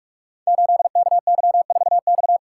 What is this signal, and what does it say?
Morse code, heard as a keyed tone, about 44 wpm, 700 Hz. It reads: 8KY4X
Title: Code Groups - 5 Letter:Numbers - 45wpm